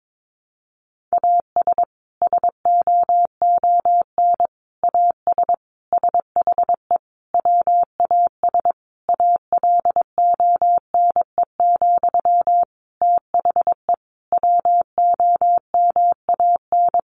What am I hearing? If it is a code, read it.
Morse code, heard as a keyed tone, about 22 wpm, 700 Hz. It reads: AS SOON AS SHE WAS ALONE, THE WOMAN
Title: Sentences from Top 1100 Words 22wpm